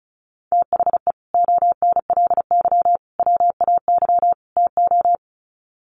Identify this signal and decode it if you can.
Morse code, heard as a keyed tone, about 34 wpm, 700 Hz. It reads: THE ONLY WAY TO
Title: Sets of 4 Words 35wpm